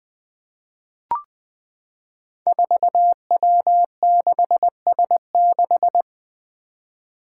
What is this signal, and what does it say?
Morse code, heard as a keyed tone, about 20 wpm, 700 Hz. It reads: 4W6S6
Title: Five Challenging Letter-Number Combinations 20wpm